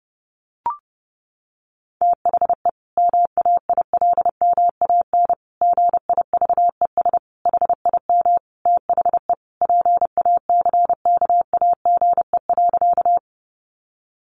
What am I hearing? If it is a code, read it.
Morse code, heard as a keyed tone, about 30 wpm, 700 Hz. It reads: THE MAILMAN GIVES HIM THE PACKAGE.
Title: Additional Easy Phrases 30wpm